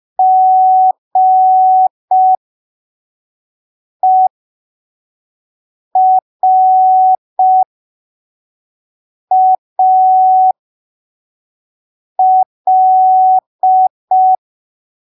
Morse code, heard as a keyed tone, about 5 wpm, spaced as for 3 wpm, 750 Hz. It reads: GERAL
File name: arquivos / texto / texto 01 5ppm